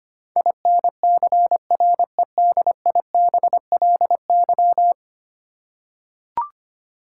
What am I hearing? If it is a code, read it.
Morse code, heard as a keyed tone, about 25 wpm, 700 Hz. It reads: INCREDIBLY
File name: Ten Letter Words - 25wpm